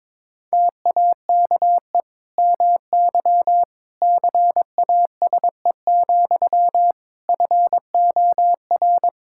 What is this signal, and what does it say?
Morse code, heard as a keyed tone, about 22 wpm, 700 Hz. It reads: TAKE MY CASE, FOR
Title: Sentences from Top 1000 Words - Review Part 5 - 22wpm